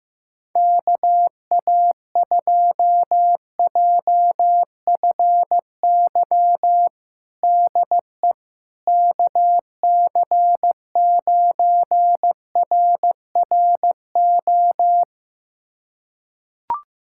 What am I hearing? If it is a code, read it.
Morse code, heard as a keyed tone, about 15 wpm, 700 Hz. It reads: KA2JFY DE KC9RRO
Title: USA Callsign DE Callsign - 6 Characters - 15wpm